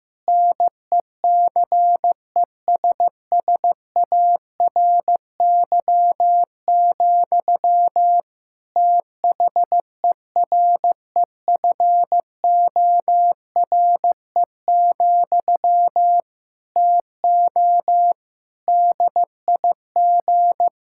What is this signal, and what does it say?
Morse code, heard as a keyed tone, about 15 wpm, 700 Hz. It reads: NECESSARY, THEREFORE, TO DIG
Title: Sentences from Top 1600 Words 15wpm